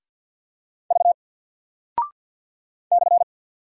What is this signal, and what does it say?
Morse code, heard as a keyed tone, about 48 wpm, 700 Hz. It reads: V /
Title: Single Character (Letter:Number:Punctuation:Prosign) 50wpm